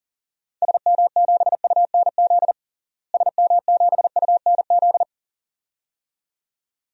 Morse code, heard as a keyed tone, about 40 wpm, 700 Hz. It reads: SM7UNZ SM7UNZ
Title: DXCC-100 Call Signs - 6 Characters - Repeated 2x - 40wpm